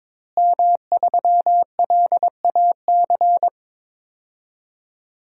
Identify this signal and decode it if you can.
Morse code, heard as a keyed tone, about 22 wpm, 700 Hz. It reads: M3LAC
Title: DXCC-100 Call Signs - 5 Characters - Repeated 3x - 22wpm